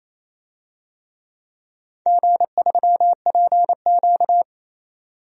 Morse code, heard as a keyed tone, about 28 wpm, 700 Hz. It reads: G3PQ
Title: DXCC-100 Call Signs - 4 Characters - 28wpm